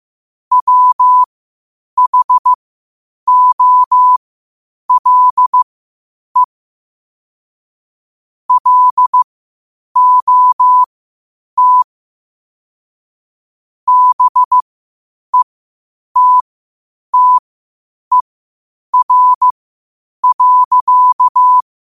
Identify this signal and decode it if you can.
Morse code, heard as a keyed tone, about 15 wpm, spaced as for 8 wpm, 1000 Hz. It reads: WHOLE LOT BETTER.
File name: qotdfiles / 8wpm